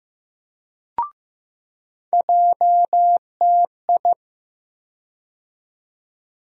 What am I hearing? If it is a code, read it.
Morse code, heard as a keyed tone, about 15 wpm, 700 Hz. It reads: JTI
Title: USA Call Sign Postfixes - 3 Letters 15wpm